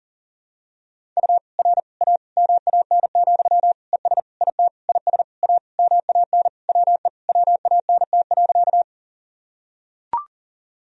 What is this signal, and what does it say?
Morse code, heard as a keyed tone, about 40 wpm, 700 Hz. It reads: U R A MAN, ES IT IS A MAN WE WANT.
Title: Sentences from Top 200 Words and Abbreviations 40wpm